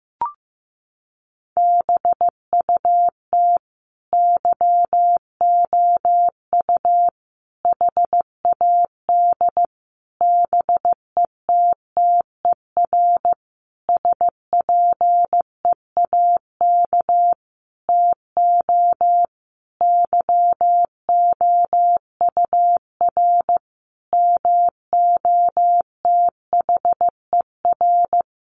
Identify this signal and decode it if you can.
Morse code, heard as a keyed tone, about 14 wpm, 700 Hz. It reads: BUT YOU HAD BETTER SPEAK TO YOUR MOTHER
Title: Sentences from Top 1100 Words 15wpm